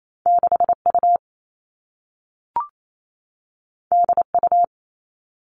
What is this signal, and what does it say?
Morse code, heard as a keyed tone, about 28 wpm, 700 Hz. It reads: 6U DU
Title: All Cut Numbers - 2-Digit 28wpm